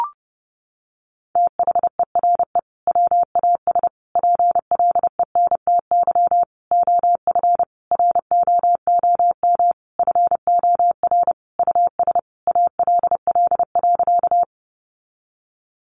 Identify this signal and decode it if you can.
Morse code, heard as a keyed tone, about 30 wpm, 700 Hz. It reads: THERE WAS PLENTY OF ROOM FOR US ALL.
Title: Sentences from Top 1300 Words 30wpm